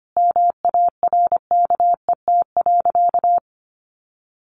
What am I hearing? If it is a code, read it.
Morse code, heard as a keyed tone, about 25 wpm, 700 Hz. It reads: MARKET.
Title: Sentences from Top 1400 Words 25wpm